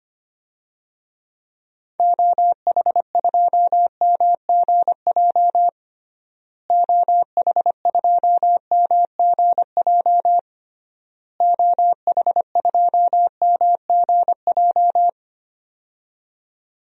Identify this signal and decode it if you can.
Morse code, heard as a keyed tone, about 25 wpm, 700 Hz. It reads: OH2MGJ OH2MGJ OH2MGJ
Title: DXCC-100 Call Signs - 6 Characters - Repeated 3x - 25wpm